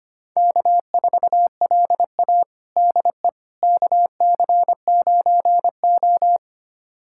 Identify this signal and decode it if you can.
Morse code, heard as a keyed tone, about 25 wpm, 700 Hz. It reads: K4LA DE KC9O
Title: USA Callsign DE Callsign - 4 Characters - 25wpm